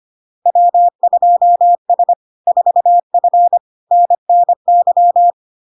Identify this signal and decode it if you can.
Morse code, heard as a keyed tone, about 25 wpm, 700 Hz. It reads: W2S 4F NNY
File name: Log-in-CW-25wpm-700-Hz